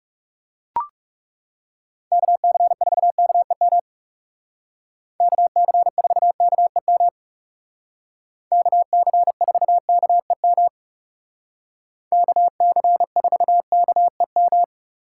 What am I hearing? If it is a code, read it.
Morse code, heard as a keyed tone, about 37 wpm, 700 Hz. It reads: KC4KEM KC4KEM KC4KEM KC4KEM
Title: Speed Racing - USA 2x3 Call Signs 30wpm